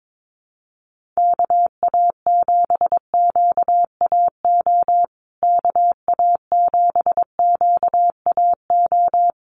Morse code, heard as a keyed tone, about 22 wpm, 700 Hz. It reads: KA7QAO KA7QAO
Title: USA Call Signs Repeated 2x 22wpm